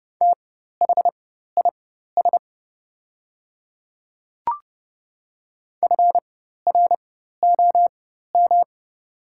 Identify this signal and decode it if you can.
Morse code, heard as a keyed tone, about 32 wpm, spaced as for 14 wpm, 700 Hz. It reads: THIS FROM
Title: 4x Character Spacing - Top 500 Words - 30wpm